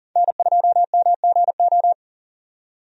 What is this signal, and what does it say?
Morse code, heard as a keyed tone, about 40 wpm, 700 Hz. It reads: N2MGO
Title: USA Call Signs - 5 Characters - 40wpm